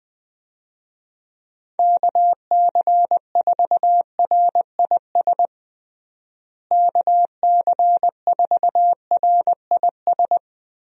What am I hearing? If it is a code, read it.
Morse code, heard as a keyed tone, about 21 wpm, 700 Hz. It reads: KC4RIS KC4RIS
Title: USA Call Signs - 6 Characters - Repeated 3x - 20wpm